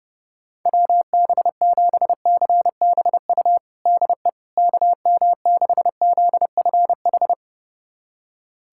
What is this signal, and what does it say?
Morse code, heard as a keyed tone, about 30 wpm, 700 Hz. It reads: WB7CBU DE KM6ZFH
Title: USA Callsign DE Callsign - 6 Characters - 30wpm